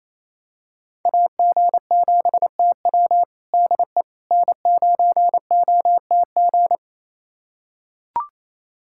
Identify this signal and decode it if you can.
Morse code, heard as a keyed tone, about 28 wpm, 700 Hz. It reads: AG7TW DE N9OTG
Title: USA Callsign DE Callsign - 5 Characters - 28wpm